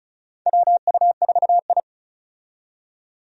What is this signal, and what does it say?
Morse code, heard as a keyed tone, about 36 wpm, 700 Hz. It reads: WU4I
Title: USA Call Signs - 4 Characters - 35wpm